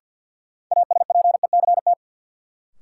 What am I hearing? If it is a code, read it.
Morse code, heard as a keyed tone, about 49 wpm, 700 Hz. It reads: ASPECT
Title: Sentences from Top 1300 Words - New Word Drill - 50wpm